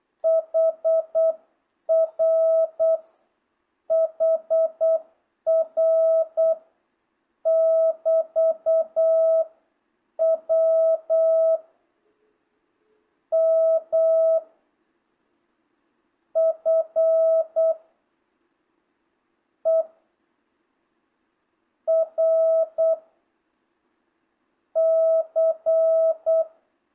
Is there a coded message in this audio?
Yes: Morse code from 0.2 to 26.4 s, heard as a keyed tone, about 8 wpm, 660 Hz. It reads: HR HR <BT> W M F E R C